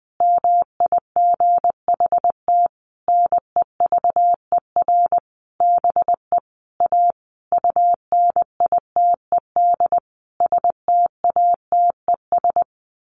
Morse code, heard as a keyed tone, about 20 wpm, 700 Hz. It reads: MIGHT NEVER BE A UNITED STATES
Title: A Sound Of Thunder 20wpm